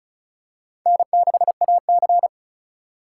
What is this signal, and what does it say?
Morse code, heard as a keyed tone, about 36 wpm, 700 Hz. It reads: N6AC
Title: USA Call Signs - 4 Characters - 4x Character Spacing - 35wpm